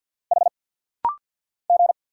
Morse code, heard as a keyed tone, about 50 wpm, 690 Hz. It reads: H G
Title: Single Letters - Rapid-Fire 50wpm